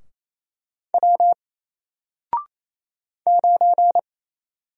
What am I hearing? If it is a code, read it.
Morse code, heard as a keyed tone, about 27 wpm, 700 Hz. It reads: W 9
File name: Single-Letter-Number-28wpm